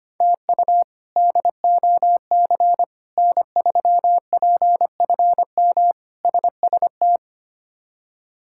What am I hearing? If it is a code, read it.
Morse code, heard as a keyed tone, about 25 wpm, 700 Hz. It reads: TU DOC N3PFM SST
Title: K1USN SST QSO Practice - 25wpm